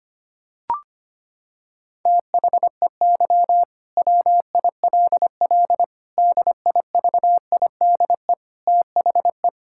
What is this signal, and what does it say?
Morse code, heard as a keyed tone, about 25 wpm, 700 Hz. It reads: THEY WILL DIVIDE THE
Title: Sentences from Top 1600 Words 25wpm